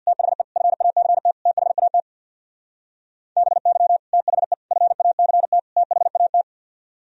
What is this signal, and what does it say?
Morse code, heard as a keyed tone, about 64 wpm, 700 Hz. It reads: THE FACT THAT BY THE FACT THAT